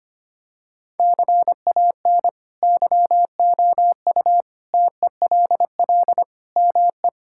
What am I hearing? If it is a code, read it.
Morse code, heard as a keyed tone, about 25 wpm, 700 Hz. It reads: CAN YOU TELL ME
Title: Sentences from Top 1600 Words 25wpm